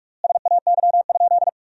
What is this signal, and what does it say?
Morse code, heard as a keyed tone, about 46 wpm, 700 Hz. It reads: SAY?
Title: 4x Word-Spacing - Sentences from Top 200 Words - 45wpm